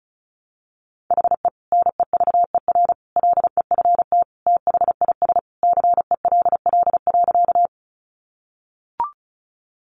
Morse code, heard as a keyed tone, about 34 wpm, 700 Hz. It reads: HE NEVER LEFT THIS CELL.
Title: Sentences from Top 1100 Words 35wpm